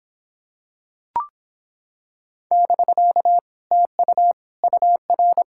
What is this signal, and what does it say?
Morse code, heard as a keyed tone, about 26 wpm, 700 Hz. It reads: <BK> TU UR